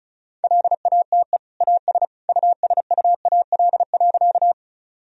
Morse code, heard as a keyed tone, about 36 wpm, 700 Hz. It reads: LATE AS USUAL.
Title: Sentences from Top 1000 Words - Review Part 2 - 35wpm